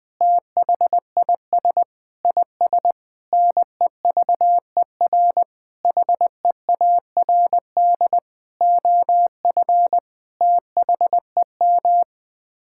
Morse code, heard as a keyed tone, about 20 wpm, 700 Hz. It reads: THIS IS NEVER HEARD OF THEM
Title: Sets of 9 Words - 20wpm